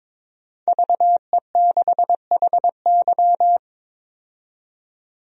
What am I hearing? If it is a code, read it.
Morse code, heard as a keyed tone, about 21 wpm, 700 Hz. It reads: VE6HY
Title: DXCC-100 Call Signs - 5 Characters - 22wpm